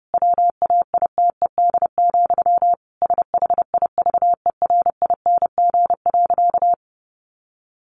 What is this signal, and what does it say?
Morse code, heard as a keyed tone, about 30 wpm, 700 Hz. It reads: WAITED, SHIVERING.
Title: A Sound Of Thunder 30wpm